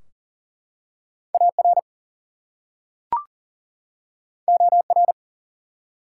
Morse code, heard as a keyed tone, about 38 wpm, 700 Hz. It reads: AR OR
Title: US States and Canadian Provinces 40wpm